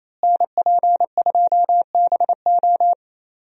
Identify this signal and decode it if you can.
Morse code, heard as a keyed tone, about 28 wpm, 700 Hz. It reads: NP2BO
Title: USA Call Signs - 5 Characters - Repeated 2x - 28wpm